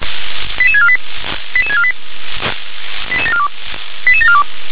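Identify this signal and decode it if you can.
Five-tone selective call (ZVEI), heard as quick runs of five tones, tones of about 70 ms. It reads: 88648 88648 88742 88742
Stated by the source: ZVEI five-tone call 88851, received on SDR.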